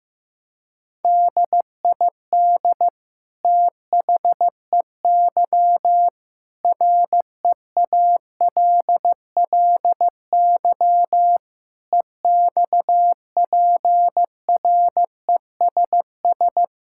Morse code, heard as a keyed tone, about 15 wpm, 700 Hz. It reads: DID THEY REALLY EXPRESS
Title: Sentences from Top 1600 Words 15wpm